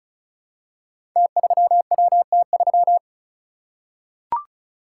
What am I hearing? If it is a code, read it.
Morse code, heard as a keyed tone, about 35 wpm, 700 Hz. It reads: T3WT3
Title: Code Groups - 5 Letter:Numbers - 35wpm